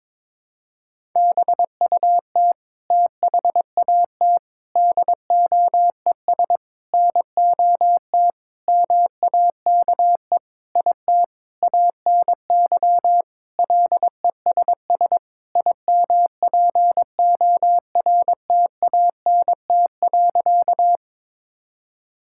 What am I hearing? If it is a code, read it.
Morse code, heard as a keyed tone, about 22 wpm, 700 Hz. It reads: BUT THAT DOES NOT MAKE IT ANY LESS IMPORTANT.
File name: Sentences with 7-Word Phrases - 22wpm